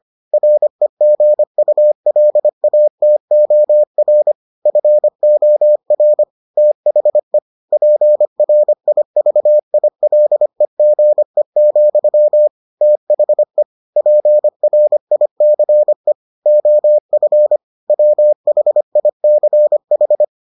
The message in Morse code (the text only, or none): REGULATOR FOR THE PRIVILEGE, THE PRICE OF WHICH